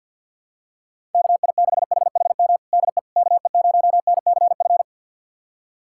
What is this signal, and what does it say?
Morse code, heard as a keyed tone, about 52 wpm, 700 Hz. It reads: KI6HHM DE KE0NCF
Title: USA Callsign DE Callsign 50wpm